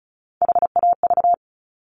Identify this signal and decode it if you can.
Morse code, heard as a keyed tone, about 36 wpm, 700 Hz. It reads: HAV
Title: Top Three Letter Combinations in Words 35wpm